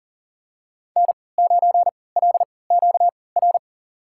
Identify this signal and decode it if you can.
Morse code, heard as a keyed tone, about 40 wpm, spaced as for 23 wpm, 700 Hz. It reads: N9LQR
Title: USA Call Signs - 3x Character Spacing - 40wpm